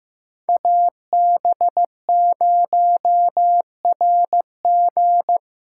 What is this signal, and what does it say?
Morse code, heard as a keyed tone, about 16 wpm, 700 Hz. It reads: AB0RG
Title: USA Call Signs - 5 Characters - 3x Character Spacing - 15wpm